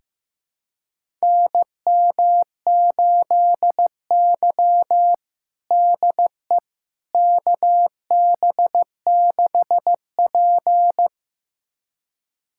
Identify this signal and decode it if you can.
Morse code, heard as a keyed tone, about 15 wpm, 700 Hz. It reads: NM8Y DE KB6P
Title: USA Callsign DE Callsign - 4 Characters - 15wpm